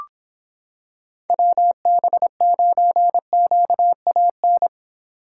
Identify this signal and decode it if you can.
Morse code, heard as a keyed tone, about 25 wpm, 700 Hz. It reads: WB9QAN